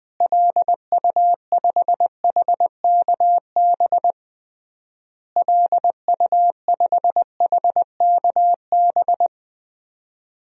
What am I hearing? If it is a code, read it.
Morse code, heard as a keyed tone, about 21 wpm, 700 Hz. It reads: LU5HKB LU5HKB
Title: DXCC-100 Call Signs - 6 Characters - Repeated 3x - 20wpm